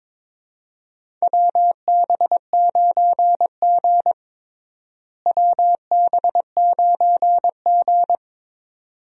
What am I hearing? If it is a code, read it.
Morse code, heard as a keyed tone, about 22 wpm, 700 Hz. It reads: WB9G WB9G
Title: USA Call Signs - 4 Characters - Repeated 2x - 22wpm